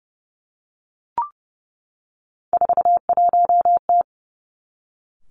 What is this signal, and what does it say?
Morse code, heard as a keyed tone, about 30 wpm, 700 Hz. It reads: 41T
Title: Three Challenging Letter-Number Combinations 30wpm